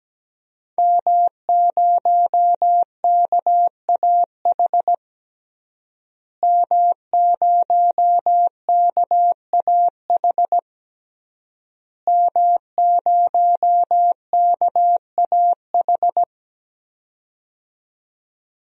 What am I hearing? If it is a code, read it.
Morse code, heard as a keyed tone, about 17 wpm, 700 Hz. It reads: M0KAH M0KAH M0KAH
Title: DXCC-100 Call Signs - 5 Characters - Repeated 3x - 17wpm